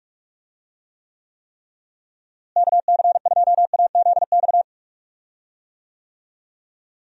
Morse code, heard as a keyed tone, about 46 wpm, 700 Hz. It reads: KC2AZX
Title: USA Call Signs - 6 Characters - Repeated 3x - 45wpm